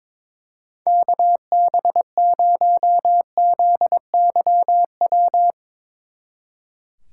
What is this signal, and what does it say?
Morse code, heard as a keyed tone, about 22 wpm, 700 Hz. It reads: KB0ZYW
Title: Speed Racing - USA 2x3 Call Signs 22wpm